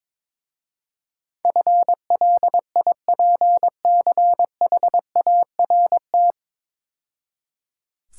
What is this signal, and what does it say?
Morse code, heard as a keyed tone, about 25 wpm, 700 Hz. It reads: FLIPCHART
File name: Easy Words - Speed-Racing with Hints - 22wpm